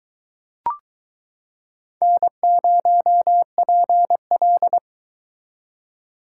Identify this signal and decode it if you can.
Morse code, heard as a keyed tone, about 22 wpm, 700 Hz. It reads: N0PL